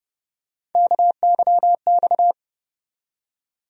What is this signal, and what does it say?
Morse code, heard as a keyed tone, about 30 wpm, 700 Hz. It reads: KYX
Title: USA Call Sign Postfixes - 3 Letters 30wpm